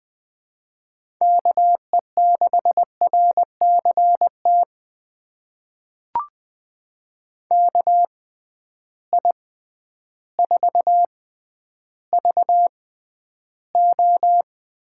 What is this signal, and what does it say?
Morse code, heard as a keyed tone, about 20 wpm, spaced as for 11 wpm, 700 Hz. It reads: KE6RCT KI4VO
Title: USA Call Signs - 6 Characters - 6x Character Spacing - 20wpm